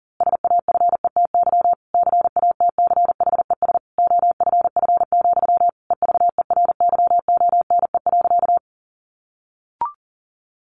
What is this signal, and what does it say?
Morse code, heard as a keyed tone, about 40 wpm, 700 Hz. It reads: SAFETY CATCHES OFF, EVERYONE.
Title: A Sound Of Thunder 40wpm